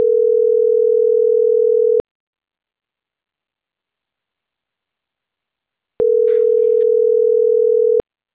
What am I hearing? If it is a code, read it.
Ringback tone (North America), 440 + 480 Hz.